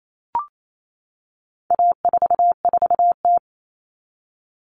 Morse code, heard as a keyed tone, about 28 wpm, 700 Hz. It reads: A44T
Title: All Cut Numbers - 4-Digit 28wpm